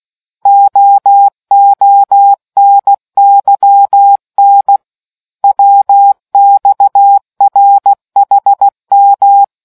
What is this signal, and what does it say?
Morse code, heard as a keyed tone, about 16 wpm, 800 Hz. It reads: OONYN WXRHM